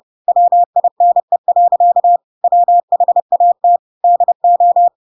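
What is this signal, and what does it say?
Morse code, heard as a keyed tone, about 30 wpm, 700 Hz. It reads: WINE. WHAT DO
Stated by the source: Chapter 2 of The Adventures of Pinocchio read as a book in Morse code (CW) at 30 wpm, by ik1hge.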